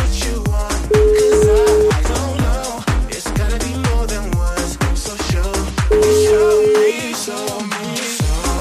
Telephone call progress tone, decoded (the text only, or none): ringback tone